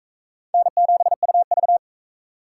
Morse code, heard as a keyed tone, about 42 wpm, 700 Hz. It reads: N7UV